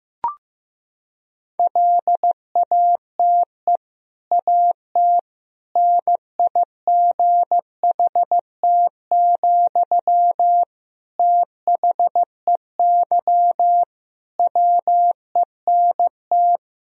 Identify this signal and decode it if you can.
Morse code, heard as a keyed tone, about 15 wpm, 700 Hz. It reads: LATE AT NIGHT, THEY WENT
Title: Sentences from Top 1000 Words - Review Part 3 - 15wpm